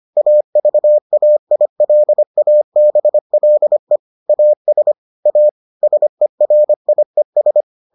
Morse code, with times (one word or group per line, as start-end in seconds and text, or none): 0.2-4.0 AVAILABLE
4.3-4.9 AS
5.3-5.5 A
5.8-7.6 SERIES